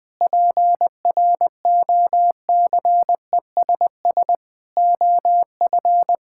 Morse code, heard as a keyed tone, about 20 wpm, 700 Hz. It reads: PROCESS OF
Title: Sentences with 4-Word Phrases - 20wpm